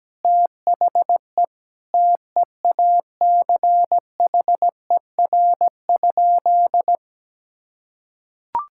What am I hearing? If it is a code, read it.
Morse code, heard as a keyed tone, about 17 wpm, 700 Hz. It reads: THE TEACHER?
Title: Sentences from Top 1000 Words - Review Part 6 - 17wpm